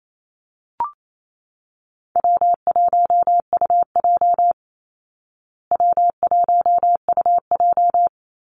Morse code, heard as a keyed tone, about 28 wpm, 700 Hz. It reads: W1UJ W1UJ
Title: Call Signs Repeated 3x - 28wpm